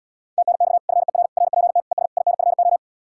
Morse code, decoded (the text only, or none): TNX FER RPT ES INFO